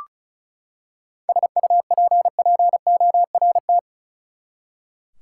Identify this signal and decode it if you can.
Morse code, heard as a keyed tone, about 34 wpm, 700 Hz. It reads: SUPPORT
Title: Seven Letter Words - 35wpm